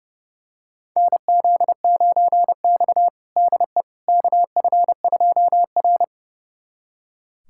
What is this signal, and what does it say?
Morse code, heard as a keyed tone, about 30 wpm, 700 Hz. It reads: NZ9X DE KF2R
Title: USA Callsign DE Callsign - 4 Characters - 30wpm